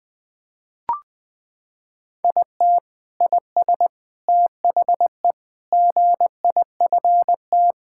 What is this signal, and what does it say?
Morse code, heard as a keyed tone, about 20 wpm, 700 Hz. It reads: IT IS THE GIFT
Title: Sentences from Top 1000 Words - Review Part 6 - 20wpm